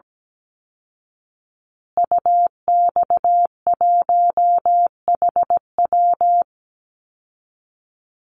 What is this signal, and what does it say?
Morse code, heard as a keyed tone, about 17 wpm, 700 Hz. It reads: UX1HW
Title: Call Signs Repeated 3x - 17wpm